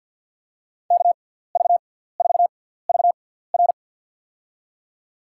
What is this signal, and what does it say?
Morse code, heard as a keyed tone, about 51 wpm, spaced as for 17 wpm, 700 Hz. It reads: KV4VR
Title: USA Call Signs - 5 Characters - 6x Character Spacing - 50wpm